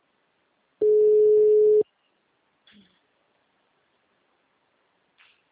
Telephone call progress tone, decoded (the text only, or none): ringback tone